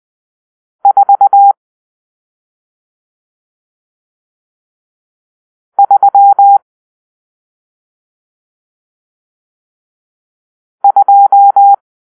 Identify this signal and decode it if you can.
Morse code, heard as a keyed tone, about 20 wpm, 800 Hz. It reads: 4 3 2
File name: arrl / Grp-6 Drill-1